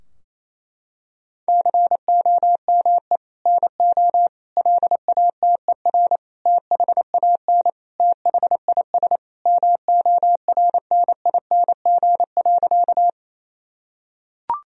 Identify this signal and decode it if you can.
Morse code, heard as a keyed tone, about 28 wpm, 700 Hz. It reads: COME NO LATER THAN THIS MORNING.
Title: Sentences from Top 1000 Words - Review Part 4 - 28wpm